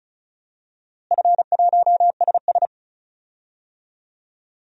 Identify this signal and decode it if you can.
Morse code, heard as a keyed tone, about 36 wpm, 700 Hz. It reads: F1SS
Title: DXCC-100 Call Signs - 4 Characters - Repeated 2x - 35wpm